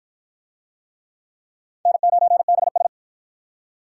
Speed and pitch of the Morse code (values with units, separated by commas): 46 wpm, 700 Hz